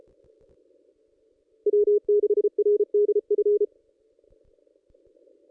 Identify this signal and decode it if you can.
Morse code, heard as a keyed tone, about 34 wpm, 410 Hz. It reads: W6RDF
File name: contests / ss-cw-k5tr-op-n5rz -1800